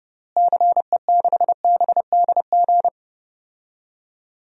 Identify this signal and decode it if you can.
Morse code, heard as a keyed tone, about 31 wpm, 700 Hz. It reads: CE6BDG
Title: DXCC-100 Call Signs - 6 Characters - Repeated 3x - 30wpm